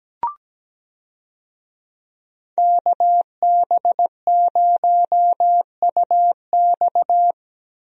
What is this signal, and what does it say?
Morse code, heard as a keyed tone, about 17 wpm, 700 Hz. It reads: KB0UX
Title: USA Call Signs - 5 Characters - 17wpm